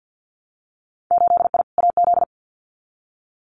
Morse code, heard as a keyed tone, about 50 wpm, 700 Hz. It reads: 8I AZ
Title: Winter Field Day Exchanges Encore 50wpm